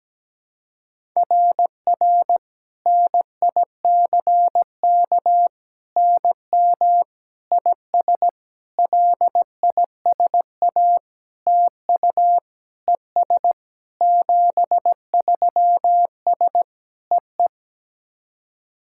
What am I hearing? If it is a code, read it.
Morse code, heard as a keyed tone, about 17 wpm, 700 Hz. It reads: RR NICK NM IS LISA TU ES 73S EE
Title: SOTA QSO Practice 17wpm